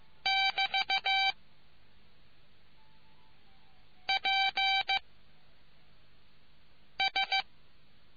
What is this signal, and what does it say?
Morse code, heard as a keyed tone, about 15 wpm, 780 Hz. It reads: <BT> P S